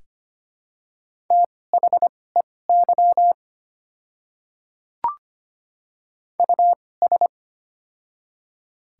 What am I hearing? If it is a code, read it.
Morse code, heard as a keyed tone, about 25 wpm, spaced as for 18 wpm, 700 Hz. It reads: THEY US
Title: 2x Character Spacing - Top 300 Words - 25wpm